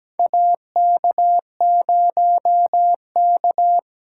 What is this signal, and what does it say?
Morse code, heard as a keyed tone, about 17 wpm, 700 Hz. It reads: AK0K
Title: USA Call Signs - 4 Characters - 2x Character Spacing - 17wpm